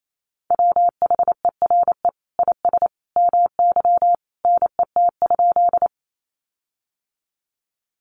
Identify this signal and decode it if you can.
Morse code, heard as a keyed tone, about 28 wpm, 700 Hz. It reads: WHERE IS MY NET?
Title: Sentences from Top 1400 Words 28wpm